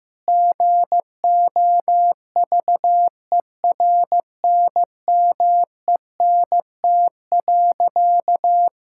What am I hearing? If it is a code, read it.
Morse code, heard as a keyed tone, about 15 wpm, 700 Hz. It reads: GOVERNMENT.
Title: Sentences from Top 1000 Words - Review Part 5 - 15wpm